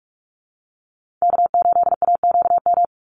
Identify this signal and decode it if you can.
Morse code, heard as a keyed tone, about 44 wpm, 700 Hz. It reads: K8AQM
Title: Call Signs Repeated 3x - 45wpm